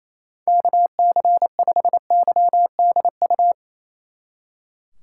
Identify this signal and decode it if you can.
Morse code, heard as a keyed tone, about 34 wpm, 700 Hz. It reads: KC5YDU
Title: Speed Racing - USA 2x3 Call Signs 28wpm